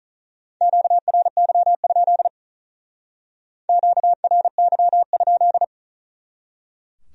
Morse code, heard as a keyed tone, about 44 wpm, 700 Hz. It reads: QRY? QRY?